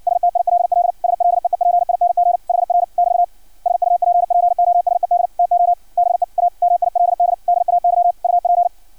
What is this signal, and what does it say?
Morse code, heard as a keyed tone, about 56 wpm, 700 Hz. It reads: STILL SLEEPING HR <BT> SUPPOSED TO BE A MILD DAY UP
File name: archived / CRSnet